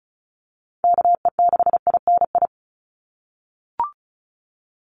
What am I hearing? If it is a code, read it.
Morse code, heard as a keyed tone, about 34 wpm, 700 Hz. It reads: KE6INI
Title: USA Call Signs 35wpm